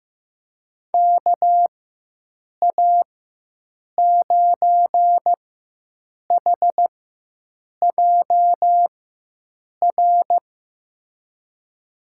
Morse code, heard as a keyed tone, about 15 wpm, spaced as for 7 wpm, 700 Hz. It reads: KA9HJR
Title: USA Call Signs - 6 Characters - 4x Character Spacing - 15wpm